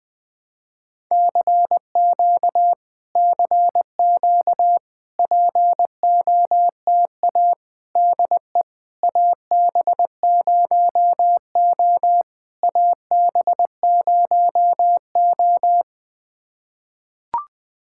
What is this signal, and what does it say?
Morse code, heard as a keyed tone, about 21 wpm, 700 Hz. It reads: CQ CQ POTA DE AB0O AB0O
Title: POTA QSO Practice - 20wpm